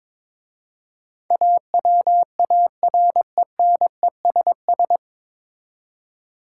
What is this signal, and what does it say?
Morse code, heard as a keyed tone, about 23 wpm, 690 Hz. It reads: AWARENESS
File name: 9 Letter Words - First 3 Letters Given - 22wpm